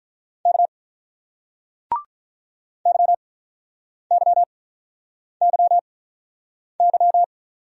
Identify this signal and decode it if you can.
Morse code, heard as a keyed tone, about 48 wpm, spaced as for 9 wpm, 700 Hz. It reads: K YYYY